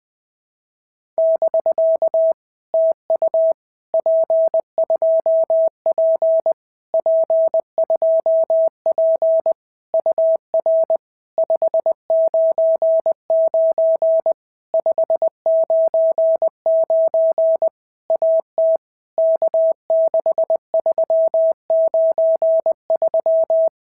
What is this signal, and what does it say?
Morse code, heard as a keyed tone, about 20 wpm, 650 Hz. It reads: <BK> TU P2P P2P UR 599 599 AT K6393
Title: POTA QSO Practice - 20wpm